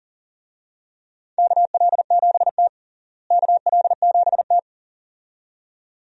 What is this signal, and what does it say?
Morse code, heard as a keyed tone, about 40 wpm, 700 Hz. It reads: KL7T KL7T
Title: USA Call Signs - 4 Characters - Repeated 2x - 40wpm